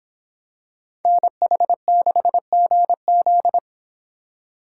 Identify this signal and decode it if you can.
Morse code, heard as a keyed tone, about 26 wpm, 700 Hz. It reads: NH6GZ